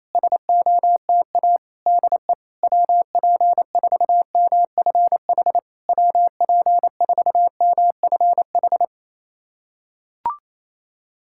Morse code, heard as a keyed tone, about 28 wpm, 700 Hz. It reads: SOTA DE WP4MFH WP4MFH
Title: SOTA QSO Practice 28wpm